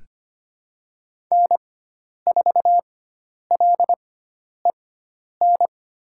Morse code, heard as a keyed tone, about 25 wpm, spaced as for 10 wpm, 700 Hz. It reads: N4LEN
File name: Lesson 16g - Letter c - Call Signs 25wpm